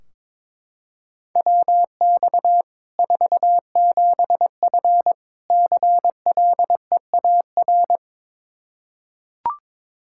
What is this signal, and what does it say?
Morse code, heard as a keyed tone, about 22 wpm, 700 Hz. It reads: WX 47F CLEAR
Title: Practice QSO - Exchange 2 - Info - 22wpm